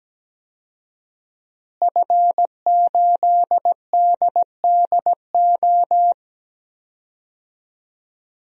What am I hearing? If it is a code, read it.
Morse code, heard as a keyed tone, about 17 wpm, 700 Hz. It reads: F8DDO
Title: DXCC-100 Call Signs - 5 Characters - Repeated 3x - 17wpm